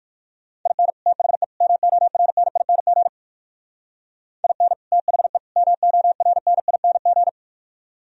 Morse code, heard as a keyed tone, about 63 wpm, 700 Hz. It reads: IN THE MORNING IN THE MORNING